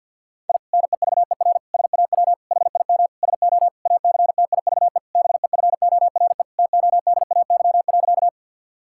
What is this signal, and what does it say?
Morse code, heard as a keyed tone, about 50 wpm, 700 Hz. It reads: I NEVER SAW HIM SO ACTIVE BEFORE TODAY.
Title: Sentences from Top 1000 Words - Review Part 6 - 50wpm